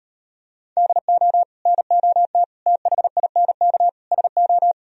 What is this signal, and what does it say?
Morse code, heard as a keyed tone, about 38 wpm, 700 Hz. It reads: DO NOT THINK SO